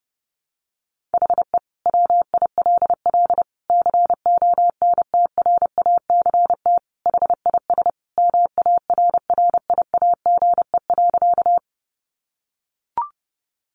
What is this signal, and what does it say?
Morse code, heard as a keyed tone, about 30 wpm, 700 Hz. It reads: HE WILL CONTRACT HIS MARRIAGE.
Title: Sentences from Top 1400 Words 30wpm